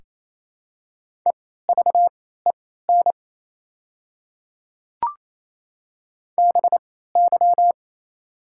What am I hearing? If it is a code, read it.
Morse code, heard as a keyed tone, about 28 wpm, spaced as for 16 wpm, 700 Hz. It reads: EVEN BY
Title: 3x Character Spacing - Top 200 Words - 28wpm